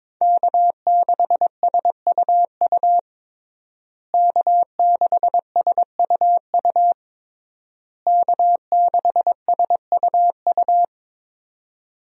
Morse code, heard as a keyed tone, about 22 wpm, 700 Hz. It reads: K6SUU K6SUU K6SUU
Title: USA Call Signs - 5 Characters - Repeated 3x - 22wpm